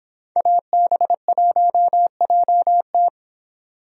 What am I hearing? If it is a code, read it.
Morse code, heard as a keyed tone, about 26 wpm, 700 Hz. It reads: AB1JT